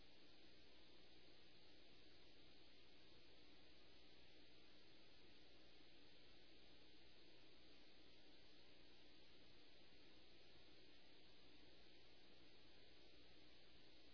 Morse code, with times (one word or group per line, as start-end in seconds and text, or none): none